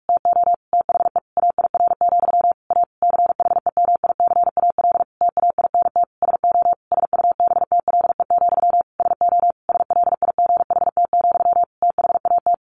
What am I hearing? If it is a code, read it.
Morse code, heard as a keyed tone, about 44 wpm, 700 Hz. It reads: TO THE AIR, A CHEMICAL TAINT SO SUBTLE, SO SLIGHT, THAT
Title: A Sound Of Thunder 45wpm